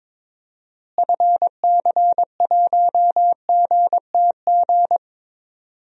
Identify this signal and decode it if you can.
Morse code, heard as a keyed tone, about 22 wpm, 700 Hz. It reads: FC1GTG
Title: DXCC-100 Call Signs - 6 Characters - Repeated 3x - 22wpm